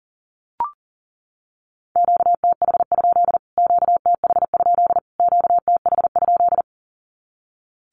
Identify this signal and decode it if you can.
Morse code, heard as a keyed tone, about 40 wpm, 700 Hz. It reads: QTH? QTH? QTH?
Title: Q-Codes Repeated 3x 40wpm